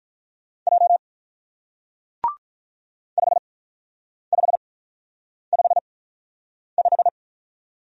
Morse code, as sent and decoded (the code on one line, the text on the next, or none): .--- / ..... ..... ..... .....
J 5555